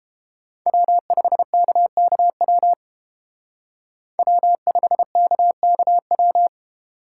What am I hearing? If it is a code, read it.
Morse code, heard as a keyed tone, about 30 wpm, 700 Hz. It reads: W5KKW W5KKW